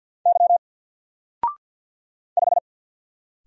Morse code, heard as a keyed tone, about 49 wpm, 700 Hz. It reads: Y 5
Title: Single Character (Letter:Number:Punctuation:Prosign) Alt 50wpm